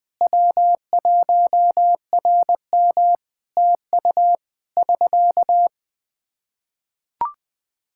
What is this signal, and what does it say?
Morse code, heard as a keyed tone, about 21 wpm, 700 Hz. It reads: W1RM TU <SK>
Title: Practice QSO - Exchange 4 - Ending - 20wpm